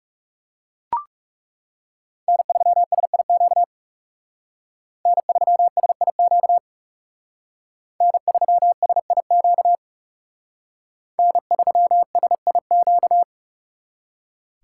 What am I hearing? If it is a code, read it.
Morse code, heard as a keyed tone, about 38 wpm, 700 Hz. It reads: N3SIQ N3SIQ N3SIQ N3SIQ